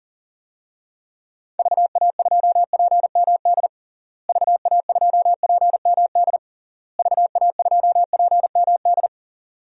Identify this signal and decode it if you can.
Morse code, heard as a keyed tone, about 42 wpm, 700 Hz. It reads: VA2PMD VA2PMD VA2PMD
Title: DXCC-100 Call Signs - 6 Characters - Repeated 3x - 40wpm